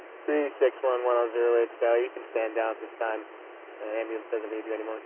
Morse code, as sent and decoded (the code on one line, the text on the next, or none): none